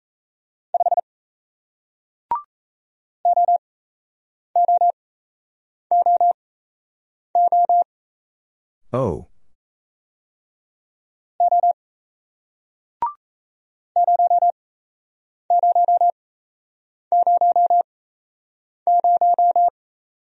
Morse code, heard as a keyed tone, about 37 wpm, spaced as for 8 wpm, 700 Hz. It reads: 5 OOOO O 0000